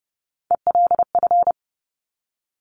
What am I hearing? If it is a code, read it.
Morse code, heard as a keyed tone, about 30 wpm, 700 Hz. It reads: ELF
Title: Top Three Letter Combinations in Words 30wpm